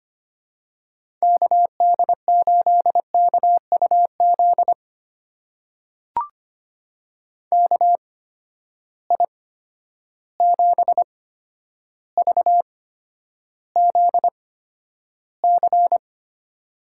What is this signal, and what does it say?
Morse code, heard as a keyed tone, about 25 wpm, spaced as for 11 wpm, 700 Hz. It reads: KD8KUZ KI7VZC
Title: USA Call Signs - 6 Characters - 8x Character Spacing - 25wpm